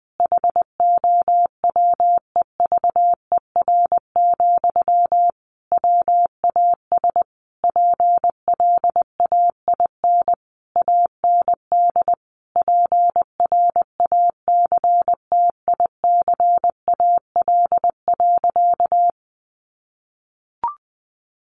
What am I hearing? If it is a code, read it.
Morse code, heard as a keyed tone, about 20 wpm, 700 Hz. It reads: HOWEVER, WAS PLAIN AND PRACTICAL.
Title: Sentences from Top 1400 Words 20wpm